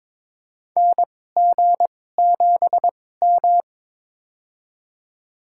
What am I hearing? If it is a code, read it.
Morse code, heard as a keyed tone, about 22 wpm, spaced as for 16 wpm, 700 Hz. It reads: NG7M
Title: USA Call Signs - 2x Character Spacing - 22wpm